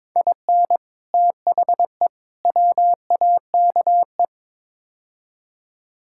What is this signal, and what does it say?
Morse code, heard as a keyed tone, about 22 wpm, 700 Hz. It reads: IN THE WAKE
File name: practice / 22wpm / Build A Phrase - 5-Word Phrases - 22wpm